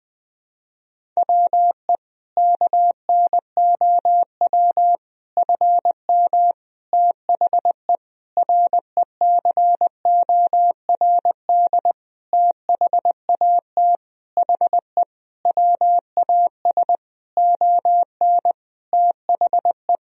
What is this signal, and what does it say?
Morse code, heard as a keyed tone, about 20 wpm, 700 Hz. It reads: WE KNOW FM THE RECORD THAT HE WAS ON THE
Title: Sentences from Top 500 Words and Abbreviations 20wpm